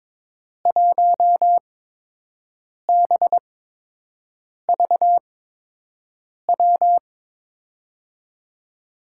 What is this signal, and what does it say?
Morse code, heard as a keyed tone, about 22 wpm, spaced as for 6 wpm, 700 Hz. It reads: 1BVW
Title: USA Call Signs - 6 Characters - 8x Character Spacing - 22wpm